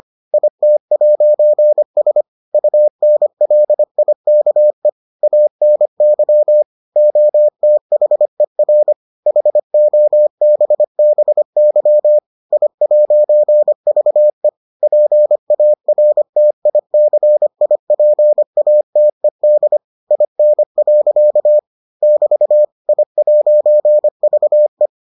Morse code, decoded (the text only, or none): IT'S UNLIKE ANY OTHER HOBBY I'VE PARTICIPATED IN. <BT> I'VE